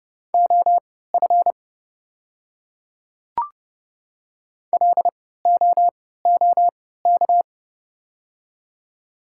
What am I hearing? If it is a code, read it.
Morse code, heard as a keyed tone, about 29 wpm, spaced as for 17 wpm, 700 Hz. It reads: OF LOOK
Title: 3x Character Spacing - Top 100 Words - 30wpm